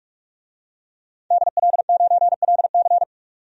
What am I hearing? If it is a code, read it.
Morse code, heard as a keyed tone, about 41 wpm, 700 Hz. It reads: DL9LC
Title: DXCC-100 Call Signs - 5 Characters - 45wpm